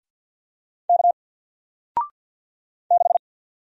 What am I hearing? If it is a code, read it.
Morse code, heard as a keyed tone, about 54 wpm, 700 Hz. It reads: K 6
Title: Single Letter-Number Alt 50wpm